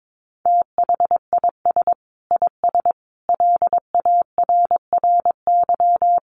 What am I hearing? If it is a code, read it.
Morse code, heard as a keyed tone, about 22 wpm, 700 Hz. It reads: THIS IS LARRY
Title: Sets of 5 Words Encore2 22wpm